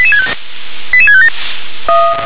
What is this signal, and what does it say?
Five-tone selective call (ZVEI), heard as quick runs of five tones, tones of about 70 ms. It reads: 88657
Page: SDR/ZVEI